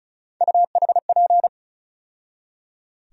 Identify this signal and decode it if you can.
Morse code, heard as a keyed tone, about 36 wpm, 700 Hz. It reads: UHP
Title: USA Call Sign Postfixes - 3 Letters 35wpm